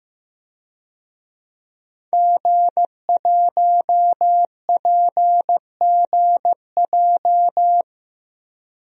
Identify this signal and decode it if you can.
Morse code, heard as a keyed tone, about 15 wpm, 700 Hz. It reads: G1PGJ
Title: DXCC-100 Call Signs - 5 Characters - 15wpm